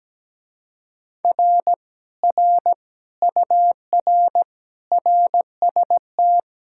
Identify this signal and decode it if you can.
Morse code, heard as a keyed tone, about 17 wpm, 700 Hz. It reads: R R UR RST
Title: Speed-Racing Common QSO Phrases - 17wpm